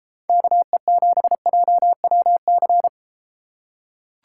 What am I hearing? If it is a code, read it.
Morse code, heard as a keyed tone, about 33 wpm, 700 Hz. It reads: KE7JWC